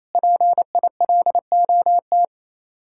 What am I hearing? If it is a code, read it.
Morse code, heard as a keyed tone, about 36 wpm, 700 Hz. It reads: PILOT